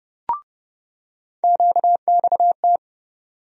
Morse code, heard as a keyed tone, about 30 wpm, 700 Hz. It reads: QXT
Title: USA Call Sign Postfixes - 3 Letters 30wpm